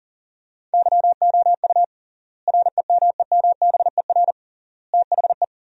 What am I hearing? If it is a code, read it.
Morse code, heard as a keyed tone, about 40 wpm, 700 Hz. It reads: YOU REMEMBER THE
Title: 3x Word-Spacing - Sentences from Top 400 Words - 40wpm